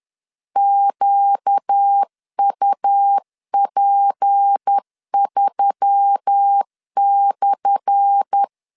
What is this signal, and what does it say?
Morse code, heard as a keyed tone, about 10 wpm, 800 Hz. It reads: QUP3/
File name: LEZIONI / 10 SIGNS / 10A SLASH 10wpm 2808